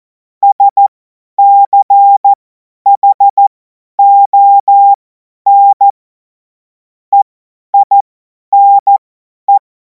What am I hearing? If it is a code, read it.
Morse code, heard as a keyed tone, about 14 wpm, spaced as for 10 wpm, 800 Hz. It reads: SCHON EINE